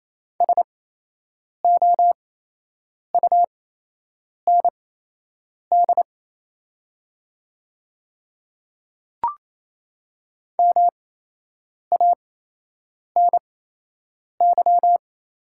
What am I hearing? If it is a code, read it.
Morse code, heard as a keyed tone, about 28 wpm, spaced as for 8 wpm, 700 Hz. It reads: SOUND MANY
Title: 8x Character Spacing - Top 200 Words - 28wpm